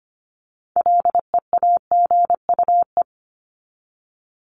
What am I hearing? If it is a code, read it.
Morse code, heard as a keyed tone, about 25 wpm, 700 Hz. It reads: LEAGUE
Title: Top 1751-2000 Words 25wpm